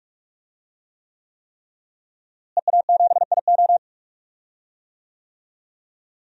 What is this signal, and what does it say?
Morse code, heard as a keyed tone, about 46 wpm, 700 Hz. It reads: EA7IO
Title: DXCC-100 Call Signs - 5 Characters - Repeated 3x - 45wpm